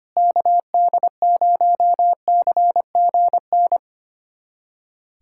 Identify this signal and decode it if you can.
Morse code, heard as a keyed tone, about 25 wpm, 700 Hz. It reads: KD0CGN
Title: USA Callsign DE Callsign - 6 Characters - 25wpm